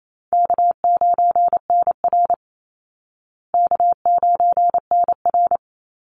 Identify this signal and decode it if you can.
Morse code, heard as a keyed tone, about 28 wpm, 700 Hz. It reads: K9NR K9NR
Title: Easy Call Signs Repeated 3x - 28wpm